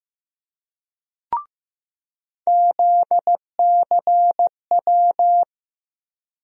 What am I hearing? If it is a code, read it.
Morse code, heard as a keyed tone, about 15 wpm, 700 Hz. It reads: ZCW
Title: USA Call Sign Postfixes - 3 Letters 15wpm